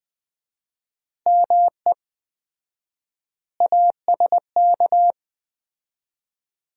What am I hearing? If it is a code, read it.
Morse code, heard as a keyed tone, about 20 wpm, 700 Hz. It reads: ME ASK
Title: 4x Word-Spacing - Sets of 5 Words - 20wpm